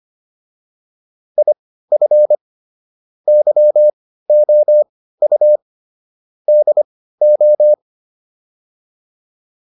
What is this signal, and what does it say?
Morse code, heard as a keyed tone, about 25 wpm, spaced as for 15 wpm, 600 Hz. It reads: IF YOU DO